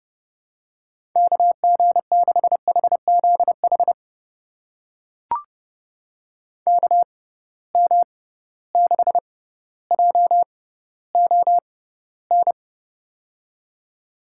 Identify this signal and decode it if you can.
Morse code, heard as a keyed tone, about 30 wpm, spaced as for 15 wpm, 700 Hz. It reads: KG6HZH KM6JON
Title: USA Call Signs - 6 Characters - 6x Character Spacing - 30wpm